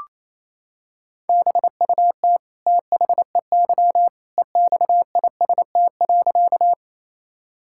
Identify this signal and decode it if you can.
Morse code, heard as a keyed tone, about 28 wpm, 700 Hz. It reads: BUT THEY EXIST.
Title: Sentences from Top 1000 Words - Review Part 5 - 28wpm